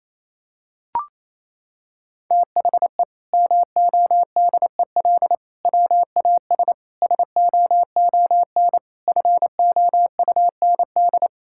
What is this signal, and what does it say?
Morse code, heard as a keyed tone, about 28 wpm, 700 Hz. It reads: THE MODEL WAS SOON FOUND
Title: Sentences from Top 1000 Words - Review Part 5 - 28wpm